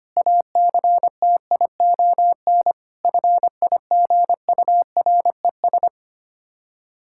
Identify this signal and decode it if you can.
Morse code, heard as a keyed tone, about 25 wpm, 700 Hz. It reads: ACTION FIGURES
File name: practice / 25wpm / Difficult Words with Hints - 25wpm